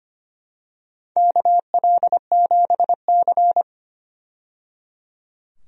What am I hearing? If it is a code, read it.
Morse code, heard as a keyed tone, about 25 wpm, 700 Hz. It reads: KL7C
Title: DXCC-100 Call Signs - 4 Characters - 25wpm